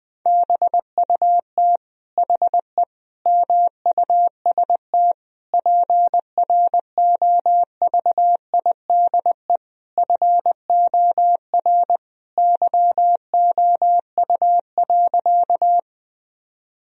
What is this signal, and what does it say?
Morse code, heard as a keyed tone, about 20 wpm, 700 Hz. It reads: BUT HE MUST PROVIDE FOR YOU.
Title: Sentences from Top 1000 Words - Review Part 3 - 20wpm